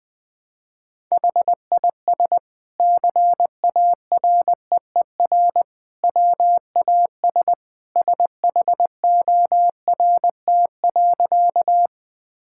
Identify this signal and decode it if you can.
Morse code, heard as a keyed tone, about 20 wpm, 700 Hz. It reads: HIS CAREER WAS SHORT.
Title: Sentences from Top 1000 Words - Review Part 6 - 20wpm